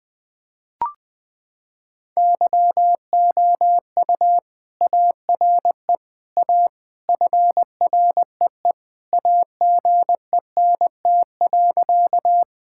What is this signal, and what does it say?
Morse code, heard as a keyed tone, about 20 wpm, 700 Hz. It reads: YOU ARE A FREE AGENT.
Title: Sentences from Top 1600 Words 20wpm